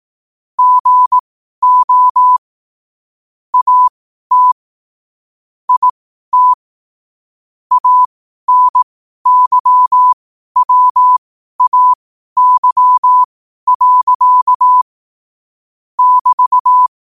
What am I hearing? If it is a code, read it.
Morse code, heard as a keyed tone, about 18 wpm, spaced as for 12 wpm, 1000 Hz. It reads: GO AT IT ANYWAY. <BT>
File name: qotdfiles / 12wpm